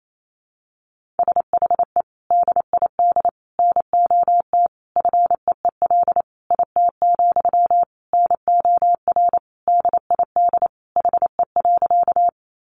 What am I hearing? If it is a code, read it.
Morse code, heard as a keyed tone, about 28 wpm, 700 Hz. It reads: SHE DID NOT FEEL IT, NOR DID HE.
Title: Sentences from Top 1500 Words 28wpm